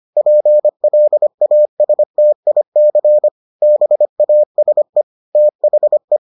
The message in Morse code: PLASTIC BASE THE

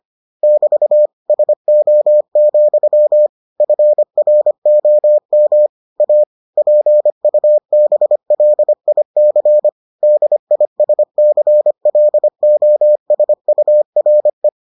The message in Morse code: <BT> SO, FROM A PUBLIC DISCLOSURE

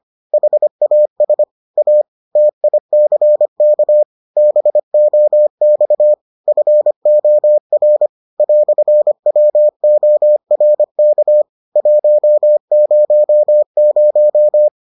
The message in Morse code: HAS A TICK BOX FOR "WORK 100

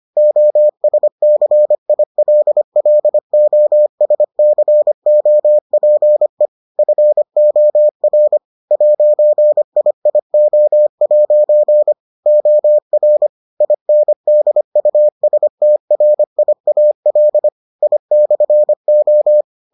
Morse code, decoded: OSCILLOSCOPE FOR 'IIO' OR INDUSTRIAL I/O

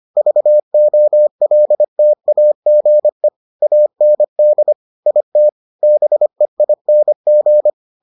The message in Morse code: VOLTAGE AND IT BEING

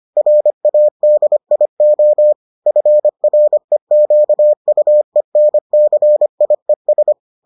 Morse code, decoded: RADIO FREQUENCIES

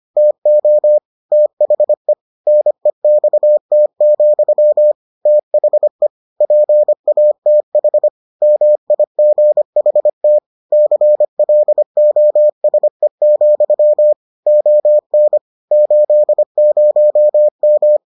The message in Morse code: TO THE NEXT, THE PATH MIGHT CLOSE, ON 80M

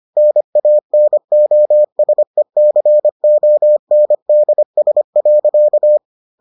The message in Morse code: NANOSECONDS.

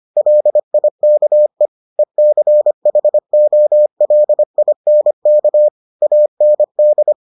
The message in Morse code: LIKE ECHOLINK AND

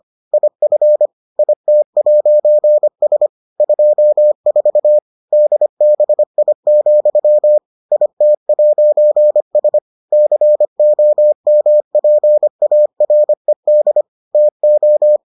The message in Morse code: IF IT'S 24 DBI, IT'S COMPARED TO

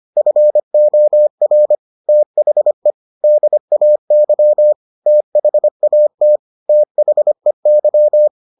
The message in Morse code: FOR THE DAY THAT THEY

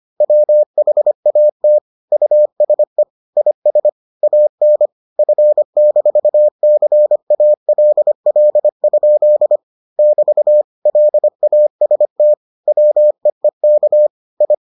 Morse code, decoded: WHAT USE IS AN F-CALL? <BT> LAST WEEK I

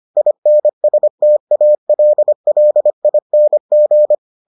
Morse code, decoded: INSTALLING

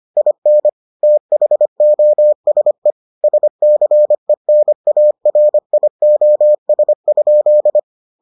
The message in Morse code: IN THOSE SCENARIOS?

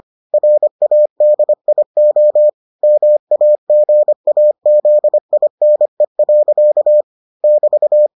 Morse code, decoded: RADIO MAGAZINE. <BT>

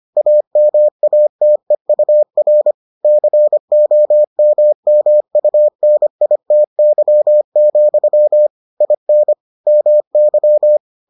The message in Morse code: AMATEUR COMMUNITY, IN MY

